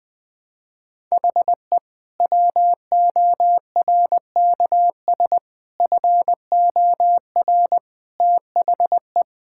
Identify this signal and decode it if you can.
Morse code, heard as a keyed tone, about 20 wpm, 700 Hz. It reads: HE WORKS FOR THE
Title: Sentences with 2-Word Phrases - Spoken Beginning - 20wpm